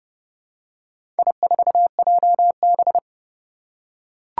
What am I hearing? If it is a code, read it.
Morse code, heard as a keyed tone, about 30 wpm, 700 Hz. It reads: I4JB
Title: Four Challenging Letter-Number Combinations 30wpm